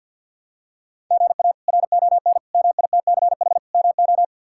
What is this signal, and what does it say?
Morse code, heard as a keyed tone, about 51 wpm, 700 Hz. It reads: GA RON MITCH MO
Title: K1USN SST QSO Practice - 50wpm